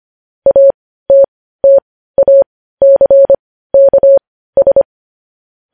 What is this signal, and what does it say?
Morse code, heard as a keyed tone, about 25 wpm, spaced as for 15 wpm, 550 Hz. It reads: ATTACKS
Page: news headlines in morse code at 15 wpm